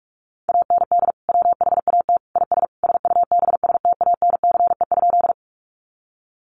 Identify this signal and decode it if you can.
Morse code, heard as a keyed tone, about 44 wpm, 700 Hz. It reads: AND WHAT IS SUBSTANCE?
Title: Sentences from Top 1500 Words 45wpm